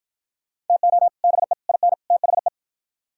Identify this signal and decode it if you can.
Morse code, heard as a keyed tone, about 52 wpm, 700 Hz. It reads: TO BE IN THE